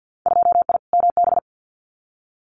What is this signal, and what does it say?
Morse code, heard as a keyed tone, about 50 wpm, 700 Hz. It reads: 2I MB
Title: Winter Field Day Exchanges Encore 50wpm